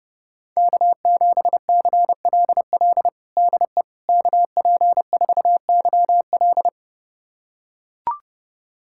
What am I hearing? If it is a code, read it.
Morse code, heard as a keyed tone, about 29 wpm, 700 Hz. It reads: K7CLL DE KP4YL
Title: USA Callsign DE Callsign - 5 Characters - 30wpm